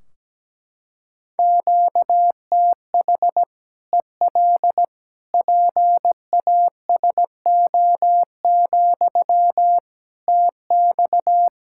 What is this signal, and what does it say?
Morse code, heard as a keyed tone, about 17 wpm, 700 Hz. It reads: QTH EL PASO, TX
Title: Practice QSO - Exchange 1 - Beginning - 17wpm